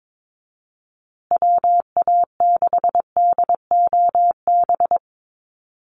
Morse code, heard as a keyed tone, about 22 wpm, 700 Hz. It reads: WA6DOB
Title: USA Call Signs Repeated 2x 22wpm